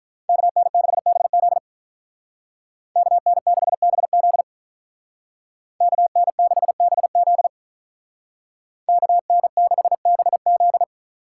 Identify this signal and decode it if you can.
Morse code, heard as a keyed tone, about 44 wpm, 700 Hz. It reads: KN6BZ KN6BZ KN6BZ KN6BZ
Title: Speed Racing - USA Call Signs 35wpm